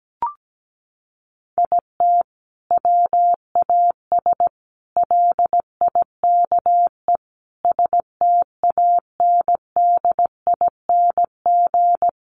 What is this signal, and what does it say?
Morse code, heard as a keyed tone, about 17 wpm, 700 Hz. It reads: IT WAS LIKE STANDING
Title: A Sound Of Thunder 17wpm